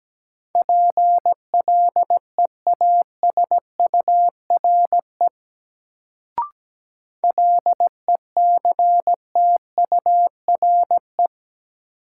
Morse code, heard as a keyed tone, about 17 wpm, 700 Hz. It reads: PLEASURE LECTURE
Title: Sentences from Top 1600 Words - New Word Drill Alt - 17wpm